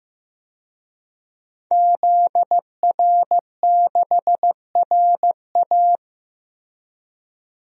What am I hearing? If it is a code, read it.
Morse code, heard as a keyed tone, about 15 wpm, 700 Hz. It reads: ZR6RA
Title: DXCC-100 Call Signs - 5 Characters - Repeated 3x - 15wpm